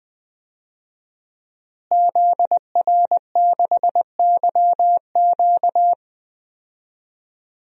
Morse code, heard as a keyed tone, about 20 wpm, 700 Hz. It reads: ZR6YQ
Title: DXCC-100 Call Signs - 5 Characters - Repeated 3x - 20wpm